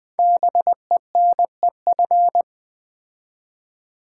Morse code, heard as a keyed tone, about 20 wpm, 700 Hz. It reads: BENEF
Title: Build A Word - 5 Letters At A Time - 10-Letter Words - 20wpm